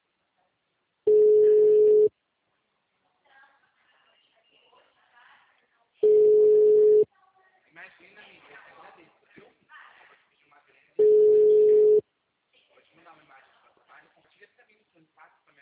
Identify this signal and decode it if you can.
Ringback tone (Europe, 425 Hz), with speech.
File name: out- -2007- - - .60983